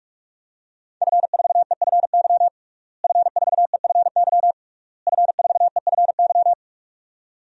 Morse code, heard as a keyed tone, about 46 wpm, 700 Hz. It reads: F4EFY F4EFY F4EFY
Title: DXCC-100 Call Signs - 5 Characters - Repeated 3x - 45wpm